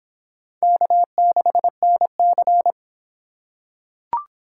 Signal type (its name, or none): Morse code